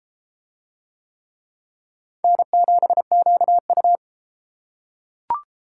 Morse code, heard as a keyed tone, about 30 wpm, 700 Hz. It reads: N7QU